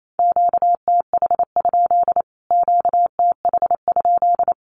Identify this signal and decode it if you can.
Morse code, heard as a keyed tone, about 28 wpm, 700 Hz. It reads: QTH? QTH?
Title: Q-Codes Repeated 3x 28wpm